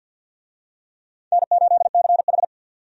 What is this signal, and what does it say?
Morse code, heard as a keyed tone, about 51 wpm, 700 Hz. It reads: N8CH
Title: USA Call Signs - 4 Characters - 50wpm